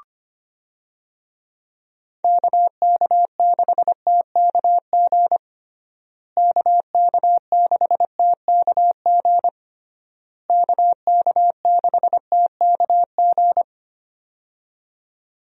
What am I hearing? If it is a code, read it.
Morse code, heard as a keyed tone, about 25 wpm, 700 Hz. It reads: KK6TKG KK6TKG KK6TKG
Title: USA Call Signs - 6 Characters - Repeated 3x - 25wpm